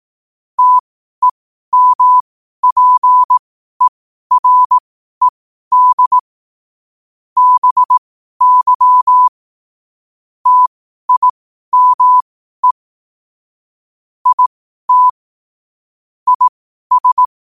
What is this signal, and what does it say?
Morse code, heard as a keyed tone, about 18 wpm, spaced as for 12 wpm, 1000 Hz. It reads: TEMPERED BY TIME IT IS
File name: qotdfiles / 12wpm